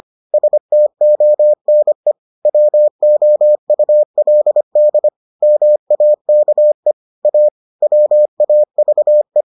Morse code, heard as a keyed tone, about 25 wpm, 600 Hz. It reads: STONE WOULD MAKE A WAVE